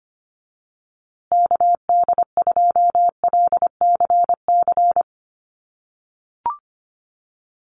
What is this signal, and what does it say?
Morse code, heard as a keyed tone, about 25 wpm, 700 Hz. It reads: KD2LCC
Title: USA Call Signs Repeated 2x 25wpm